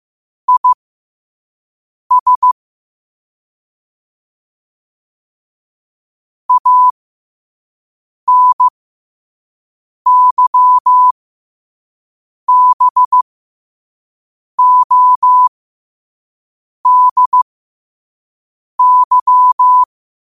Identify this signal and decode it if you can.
Morse code, heard as a keyed tone, about 15 wpm, spaced as for 5 wpm, 1000 Hz. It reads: IS ANYBODY
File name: qotdfiles / 5wpm